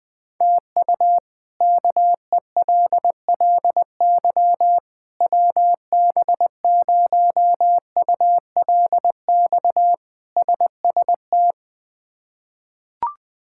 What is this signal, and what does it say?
Morse code, heard as a keyed tone, about 20 wpm, 700 Hz. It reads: TU KELLY WB0ULX SST
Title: K1USN SST QSO Practice - 20wpm